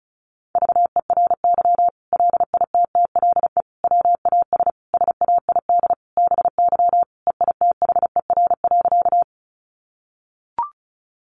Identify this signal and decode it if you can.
Morse code, heard as a keyed tone, about 34 wpm, 700 Hz. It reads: VERY LITTLE WAS SAID BY EITHER.
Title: Sentences from Top 600 Words 35wpm